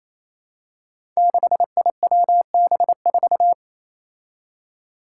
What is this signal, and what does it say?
Morse code, heard as a keyed tone, about 27 wpm, 700 Hz. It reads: 6IWB4
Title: Five Challenging Letter-Number Combinations 28wpm